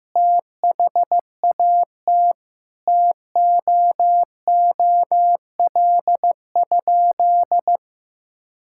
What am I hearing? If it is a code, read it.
Morse code, heard as a keyed tone, about 15 wpm, 700 Hz. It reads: THAT TOOL?
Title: Sentences from Top 1000 Words - Review Part 6 - 15wpm